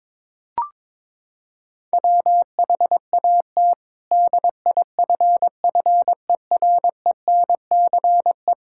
Morse code, heard as a keyed tone, about 22 wpm, 700 Hz. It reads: WHAT DIFFERENCE
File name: practice / 22wpm / Sentences with 8-Word Phrases - 22wpm